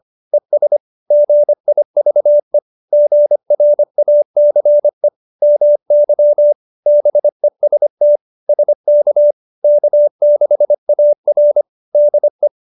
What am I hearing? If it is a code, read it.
Morse code, heard as a keyed tone, about 25 wpm, 590 Hz. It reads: ES GIVE GRACE MY BEST SK K6AR DE